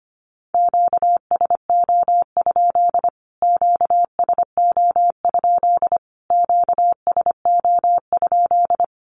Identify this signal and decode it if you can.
Morse code, heard as a keyed tone, about 25 wpm, 700 Hz. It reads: QSO? QSO? QSO?
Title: Q-Codes Repeated 3x 25wpm